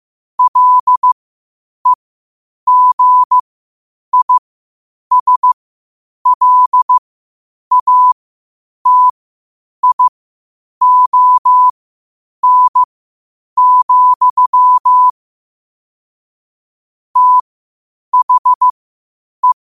Morse code, heard as a keyed tone, about 15 wpm, spaced as for 8 wpm, 1000 Hz. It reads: LEGISLATION, THE
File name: qotdfiles / 8wpm